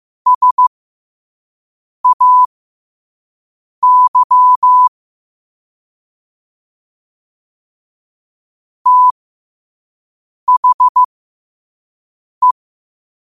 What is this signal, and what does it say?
Morse code, heard as a keyed tone, about 15 wpm, spaced as for 5 wpm, 1000 Hz. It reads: SAY THE
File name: qotdfiles / 5wpm